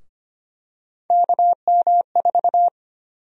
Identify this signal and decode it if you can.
Morse code, heard as a keyed tone, about 25 wpm, 700 Hz. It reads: KM4
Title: USA Call Sign Prefixes - 25wpm